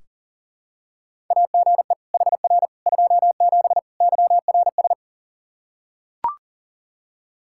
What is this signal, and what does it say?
Morse code, heard as a keyed tone, about 37 wpm, 700 Hz. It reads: AGE HR 27 YRS
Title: Practice QSO - Exchange 3 - More Info - 40wpm